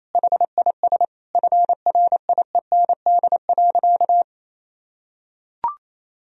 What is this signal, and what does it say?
Morse code, heard as a keyed tone, about 28 wpm, 700 Hz. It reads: HIS FRIEND.
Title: Sentences from Top 1000 Words - Review Part 4 - 28wpm